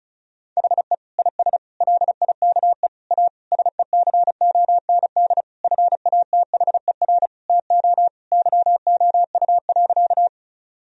Morse code, heard as a keyed tone, about 36 wpm, 700 Hz. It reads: HE IS LIKE A SECOND FATHER TO YOU.
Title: Sentences from Top 1000 Words - Review Part 1 - 35wpm